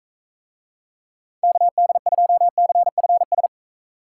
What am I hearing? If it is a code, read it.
Morse code, heard as a keyed tone, about 44 wpm, 700 Hz. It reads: KD2CFS